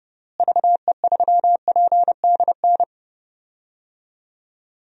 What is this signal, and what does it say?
Morse code, heard as a keyed tone, about 31 wpm, 700 Hz. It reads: VE3PDN
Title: DXCC-100 Call Signs - 6 Characters - Repeated 3x - 30wpm